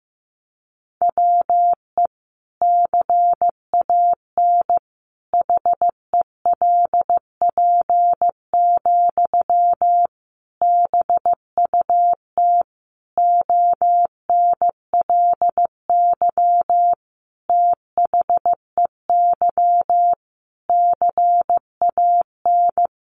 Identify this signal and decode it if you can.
Morse code, heard as a keyed tone, about 15 wpm, 700 Hz. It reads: WE CAN HELP, BUT ONLY THEY CAN
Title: Sentences from Top 1500 Words 15wpm